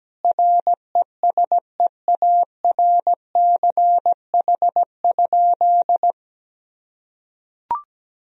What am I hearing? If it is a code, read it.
Morse code, heard as a keyed tone, about 17 wpm, 700 Hz. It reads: RESEARCH?
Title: Sentences from Top 1000 Words - Review Part 4 - 17wpm